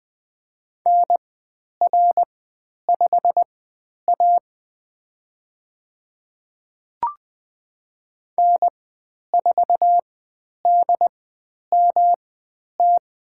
Morse code, heard as a keyed tone, about 20 wpm, spaced as for 10 wpm, 700 Hz. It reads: NR5A N4DMT
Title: 109 Lesson 18g - Letter m - Call Signs 20wpm